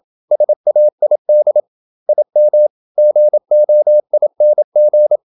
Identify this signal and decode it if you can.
Morse code, heard as a keyed tone, about 27 wpm, 600 Hz. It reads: SAID IM GOING
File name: Never 0027wpm